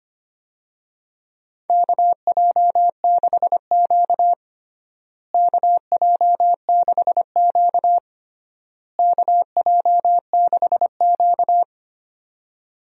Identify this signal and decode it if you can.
Morse code, heard as a keyed tone, about 25 wpm, 700 Hz. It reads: KJ6Q KJ6Q KJ6Q
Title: USA Call Signs - 4 Characters - Repeated 3x - 25wpm